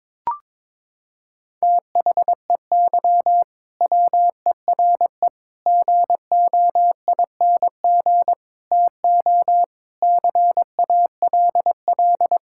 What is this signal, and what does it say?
Morse code, heard as a keyed tone, about 22 wpm, 700 Hz. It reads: THEY WERE GOING TO CALL
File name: practice / 22wpm / Sentences with 4-Word Phrases - 22wpm